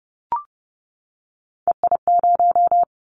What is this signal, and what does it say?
Morse code, heard as a keyed tone, about 30 wpm, 700 Hz. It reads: EI0
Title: Callsign Prefixes for Top 50 DXCC - 30wpm